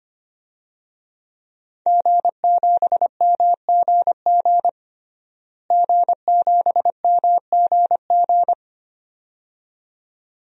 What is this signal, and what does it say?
Morse code, heard as a keyed tone, about 25 wpm, 700 Hz. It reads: G7MGG G7MGG
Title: DXCC-100 Call Signs - 5 Characters - Repeated 2x - 25wpm